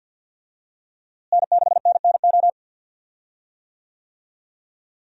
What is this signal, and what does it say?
Morse code, heard as a keyed tone, about 53 wpm, 700 Hz. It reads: N6NNO
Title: USA Call Signs - 5 Characters - Repeated 3x - 50wpm